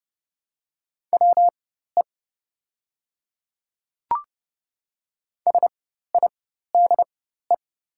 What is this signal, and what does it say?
Morse code, heard as a keyed tone, about 30 wpm, spaced as for 14 wpm, 700 Hz. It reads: WE SIDE
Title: 4x Character Spacing - Top 300 Words - 30wpm